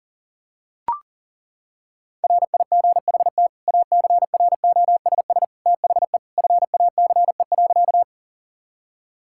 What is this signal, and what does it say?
Morse code, heard as a keyed tone, about 40 wpm, 700 Hz. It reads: RIGHT ACROSS THE FACE.
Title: Sentences from Top 1000 Words - Review Part 5 - 40wpm